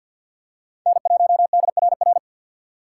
Morse code, heard as a keyed tone, about 52 wpm, 700 Hz. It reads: N1DRR
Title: USA Call Signs - 5 Characters - 6x Character Spacing - 50wpm